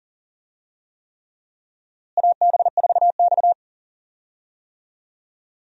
Morse code, heard as a keyed tone, about 40 wpm, 700 Hz. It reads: AB4X
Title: USA Call Signs - 4 Characters - Repeated 3x - 40wpm